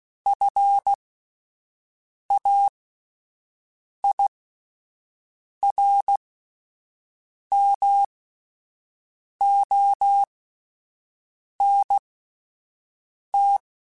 Morse code, heard as a keyed tone, about 16 wpm, spaced as for 5 wpm, 800 Hz. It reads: FAIRMONT